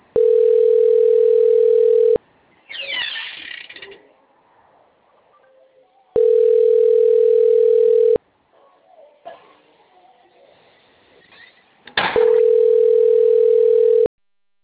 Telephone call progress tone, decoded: ringback tone

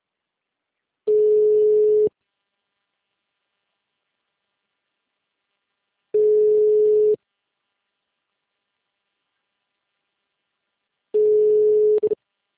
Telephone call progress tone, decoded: ringback tone

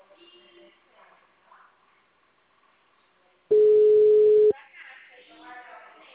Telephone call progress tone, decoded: ringback tone

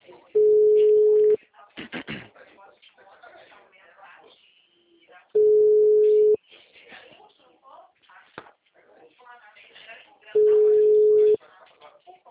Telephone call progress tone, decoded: ringback tone

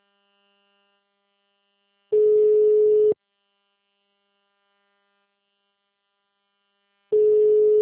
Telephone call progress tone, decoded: ringback tone

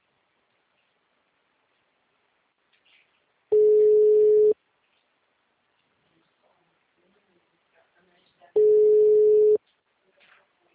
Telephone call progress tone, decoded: ringback tone

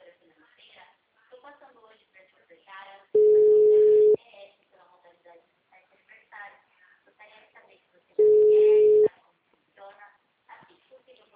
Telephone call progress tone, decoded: ringback tone

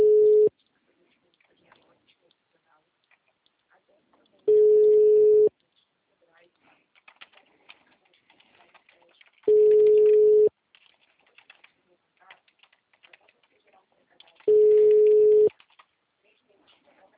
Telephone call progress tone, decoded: ringback tone

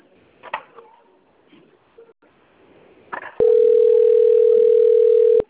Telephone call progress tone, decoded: ringback tone

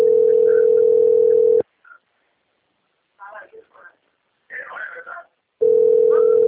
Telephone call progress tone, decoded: ringback tone

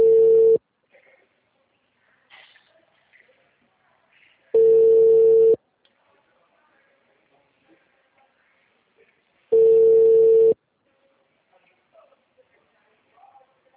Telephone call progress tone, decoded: ringback tone

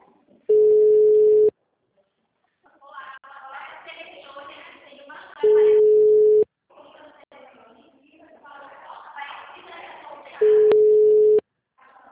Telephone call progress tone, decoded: ringback tone

